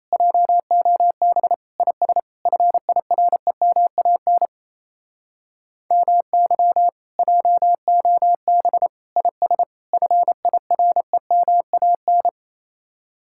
Morse code, heard as a keyed tone, about 38 wpm, 700 Hz. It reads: JOB IS FIREMAN MY JOB IS FIREMAN